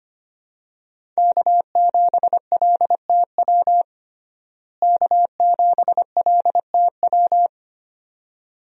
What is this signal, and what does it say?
Morse code, heard as a keyed tone, about 25 wpm, 700 Hz. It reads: K7LTW K7LTW
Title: USA Call Signs - 5 Characters - Repeated 2x - 25wpm